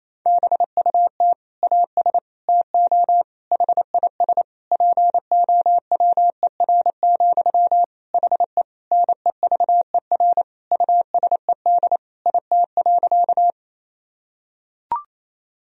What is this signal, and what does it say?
Morse code, heard as a keyed tone, about 28 wpm, 700 Hz. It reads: BUT AS TO HIS POWER, HE NEVER USED IT.
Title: Sentences from Top 1000 Words - Review Part 2 - 28wpm